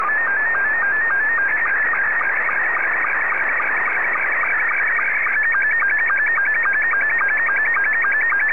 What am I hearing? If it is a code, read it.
SSTV (slow-scan television), Scottie 2.